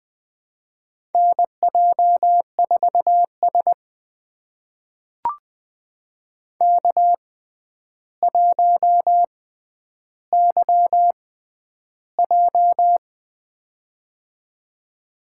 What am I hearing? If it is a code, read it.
Morse code, heard as a keyed tone, about 20 wpm, spaced as for 10 wpm, 700 Hz. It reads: NJ4S K1YJ
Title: USA Call Signs - 4 Characters - 6x Character Spacing - 20wpm